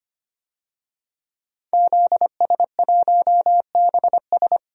Morse code, heard as a keyed tone, about 25 wpm, 700 Hz. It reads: ZS1BS
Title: DXCC-100 Call Signs - 5 Characters - 25wpm